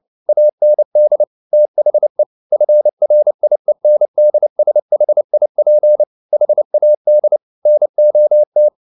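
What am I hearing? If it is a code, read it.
Morse code, heard as a keyed tone, about 29 wpm, 600 Hz. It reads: AND THE FRIENDSHIP HAD NOT
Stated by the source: Morse code at 29 wpm, sending Ken Follett's novel Never in English, from CW med Gnister.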